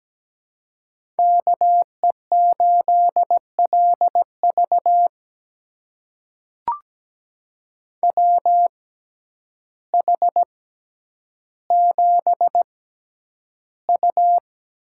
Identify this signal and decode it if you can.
Morse code, heard as a keyed tone, about 17 wpm, spaced as for 9 wpm, 700 Hz. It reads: KE8LV WH7U
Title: USA Call Signs - 5 Characters - 6x Character Spacing - 17wpm